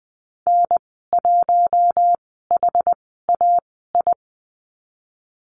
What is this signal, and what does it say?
Morse code, heard as a keyed tone, about 20 wpm, spaced as for 15 wpm, 700 Hz. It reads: N1HAI
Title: 238 Lesson 38g - Letter q - Call Signs 20wpm